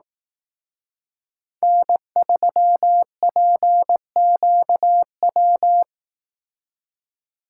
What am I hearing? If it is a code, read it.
Morse code, heard as a keyed tone, about 18 wpm, 700 Hz. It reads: N3PQW
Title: Speed Racing - USA 1x3 Call Signs 15wpm